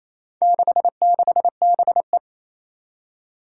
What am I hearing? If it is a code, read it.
Morse code, heard as a keyed tone, about 27 wpm, 700 Hz. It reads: 66BE
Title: Four Challenging Letter-Number Combinations 28wpm